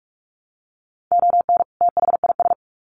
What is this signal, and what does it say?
Morse code, heard as a keyed tone, about 44 wpm, 700 Hz. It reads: ON THIS
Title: Sets of 2 Words 45wpm